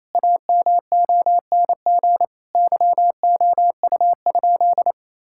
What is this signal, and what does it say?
Morse code, heard as a keyed tone, about 28 wpm, 700 Hz. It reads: AMONG YOU?
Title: Sentences from Top 1600 Words 28wpm